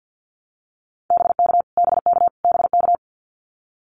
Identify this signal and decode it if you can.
Morse code, heard as a keyed tone, about 50 wpm, 700 Hz. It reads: BK BK BK
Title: Abbreviations Repeated 3x - 50wpm